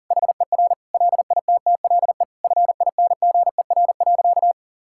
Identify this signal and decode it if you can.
Morse code, heard as a keyed tone, about 37 wpm, 700 Hz. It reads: HER LITTLE FINGER.
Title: Sentences from Top 1600 Words 40wpm